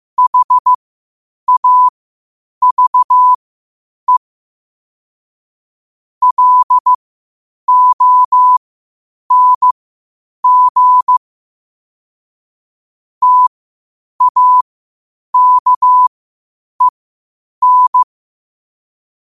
Morse code, heard as a keyed tone, about 15 wpm, spaced as for 8 wpm, 1000 Hz. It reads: HAVE LONG TAKEN
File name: qotdfiles / 8wpm